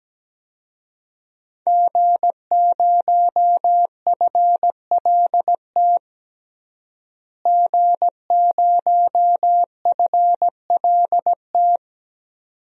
Morse code, heard as a keyed tone, about 17 wpm, 700 Hz. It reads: G0FLT G0FLT
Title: DXCC-100 Call Signs - 5 Characters - Repeated 2x - 17wpm